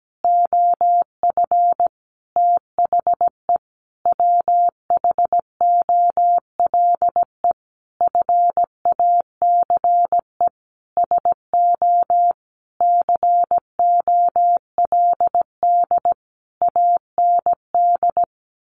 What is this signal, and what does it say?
Morse code, heard as a keyed tone, about 17 wpm, 700 Hz. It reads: OF THE WHOLE FACE SO COLD AND
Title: Sentences from Top 1400 Words 17wpm